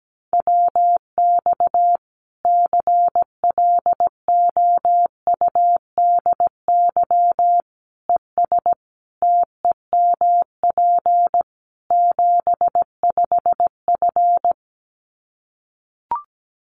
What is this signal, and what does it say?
Morse code, heard as a keyed tone, about 17 wpm, 700 Hz. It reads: WX CLOUDY ES TEMP 75F
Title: Common QSO Phrases 17wpm